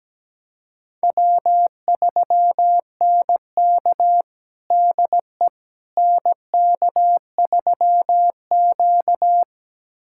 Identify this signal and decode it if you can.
Morse code, heard as a keyed tone, about 17 wpm, 700 Hz. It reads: W3NK DE NK3Q
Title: USA Callsign DE Callsign - 4 Characters - 17wpm